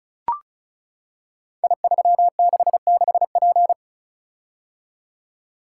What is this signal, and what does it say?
Morse code, heard as a keyed tone, about 36 wpm, 700 Hz. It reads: I366P
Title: Code Groups - 5 Letter:Numbers - 35wpm